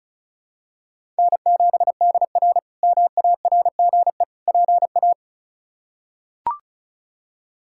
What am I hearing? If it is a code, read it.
Morse code, heard as a keyed tone, about 36 wpm, 700 Hz. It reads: N7DR MARGE PA
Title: K1USN SST QSO Practice - 35wpm